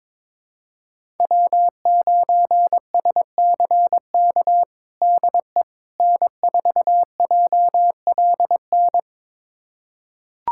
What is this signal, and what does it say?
Morse code, heard as a keyed tone, about 22 wpm, 700 Hz. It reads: W9SCK DE N4JLN
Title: USA Callsign DE Callsign - 5 Characters - 22wpm